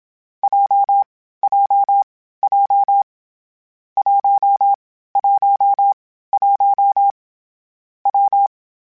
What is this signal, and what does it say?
Morse code, heard as a keyed tone, about 27 wpm, 800 Hz. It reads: J J J 1 1 1 W
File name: Variable Speed / Variable Speed-mp3s / WPJ WPJ1 sp @25